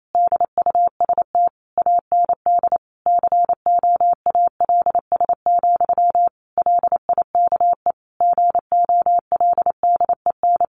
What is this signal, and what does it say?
Morse code, heard as a keyed tone, about 28 wpm, 700 Hz. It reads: DUST AND COALS, LIKE GOLDEN
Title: A Sound Of Thunder 28wpm